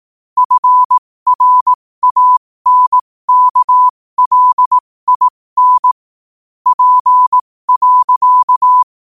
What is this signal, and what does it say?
Morse code, heard as a keyed tone, about 18 wpm, spaced as for 15 wpm, 1000 Hz. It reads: FRANKLIN P.
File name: 2024-11-18-15wpm